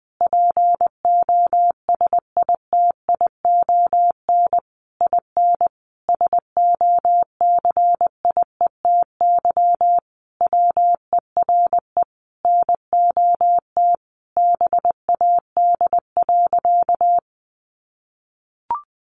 Morse code, heard as a keyed tone, about 20 wpm, 700 Hz. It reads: POSITION IN SOCIETY WERE NOT BAD.
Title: Sentences from Top 900 Words 20wpm